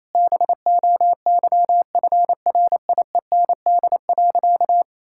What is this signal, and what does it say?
Morse code, heard as a keyed tone, about 27 wpm, 700 Hz. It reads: BOYFRIEND.
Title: Sentences from Top 1000 Words - Review Part 5 - 28wpm